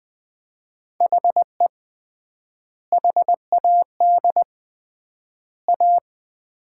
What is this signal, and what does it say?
Morse code, heard as a keyed tone, about 20 wpm, 700 Hz. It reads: HE HAD A
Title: 3x Word-Spacing - Sets of 3 Words - 20wpm